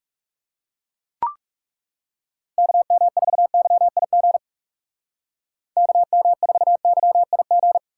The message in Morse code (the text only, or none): KM4YIG KM4YIG